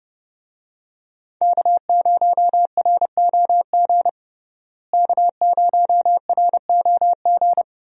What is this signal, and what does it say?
Morse code, heard as a keyed tone, about 30 wpm, 700 Hz. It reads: K0ROG K0ROG
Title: USA Call Signs - 5 Characters - Repeated 3x - 30wpm